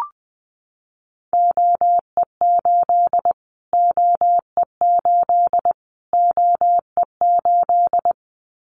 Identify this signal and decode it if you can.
Morse code, heard as a keyed tone, about 20 wpm, 700 Hz. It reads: OE8 OE8 OE8
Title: Callsign Prefixes for Top 25 DXCC - Repeated 3x - 20wpm